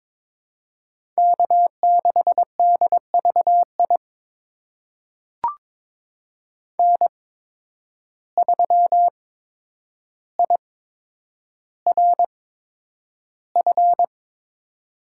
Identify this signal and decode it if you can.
Morse code, heard as a keyed tone, about 23 wpm, spaced as for 10 wpm, 700 Hz. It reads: K6DVI N3IRF
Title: USA Call Signs - 5 Characters - 8x Character Spacing - 22wpm